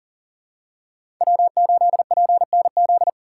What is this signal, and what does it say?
Morse code, heard as a keyed tone, about 40 wpm, 700 Hz. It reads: W8PNZ
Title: USA Call Signs - 5 Characters - 3x Character Spacing - 40wpm